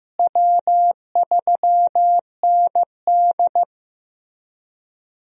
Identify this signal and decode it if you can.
Morse code, heard as a keyed tone, about 15 wpm, 700 Hz. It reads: W3ND
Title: USA Call Signs - 4 Characters - Repeated 3x - 15wpm